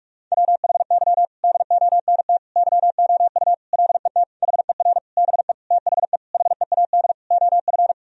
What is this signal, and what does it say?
Morse code, heard as a keyed tone, about 46 wpm, 700 Hz. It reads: WHY DONT YOU LET HER BE THE HEAD OF
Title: Sentences from Top 1000 Words - Review Part 1 - 45wpm